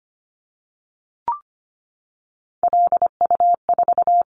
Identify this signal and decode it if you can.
Morse code, heard as a keyed tone, about 25 wpm, 700 Hz. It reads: LU4
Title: Callsign Prefixes for Top 50 DXCC - 25wpm